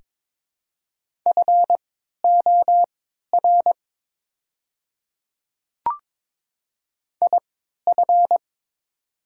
Morse code, heard as a keyed tone, about 22 wpm, spaced as for 13 wpm, 700 Hz. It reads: FOR IF
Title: 3x Character Spacing - Top 100 Words - 22wpm